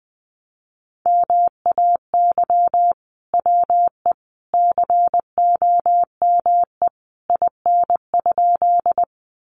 Morse code, heard as a keyed tone, about 20 wpm, 700 Hz. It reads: MAY WE COME IN?
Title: Sentences from Top 100 Words 20wpm